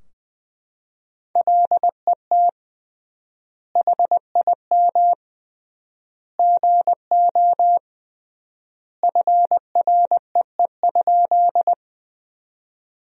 Morse code, heard as a keyed tone, about 20 wpm, 700 Hz. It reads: LET HIM GO FREE?
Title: 3x Word-Spacing - Sentences from Top 500 Words - 20wpm